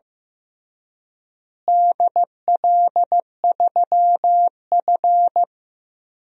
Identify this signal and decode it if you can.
Morse code, heard as a keyed tone, about 15 wpm, 700 Hz. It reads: DL3F
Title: DXCC-100 Call Signs - 4 Characters - Repeated 2x - 15wpm